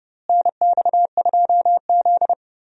Morse code, heard as a keyed tone, about 30 wpm, 700 Hz. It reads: NX2Z
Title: USA Call Signs - 4 Characters - Repeated 3x - 30wpm